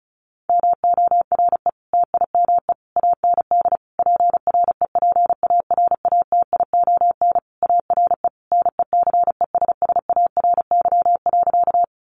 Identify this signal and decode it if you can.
Morse code, heard as a keyed tone, about 34 wpm, 700 Hz. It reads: MORE TIME AND PREPARATION ARE NECESSARY.
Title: Sentences from Top 1400 Words 35wpm